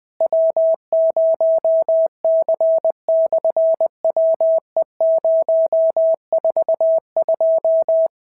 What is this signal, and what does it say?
Morse code, heard as a keyed tone, about 20 wpm, 650 Hz. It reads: W0C/WE042
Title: SOTA QSO Practice 20wpm